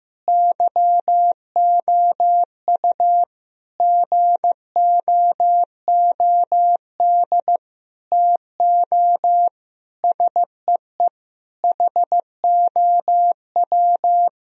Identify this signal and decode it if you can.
Morse code, heard as a keyed tone, about 15 wpm, 700 Hz. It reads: YOU GOOD TO SEE HOW
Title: Sentences from Top 1000 Words - Review Part 2 - 15wpm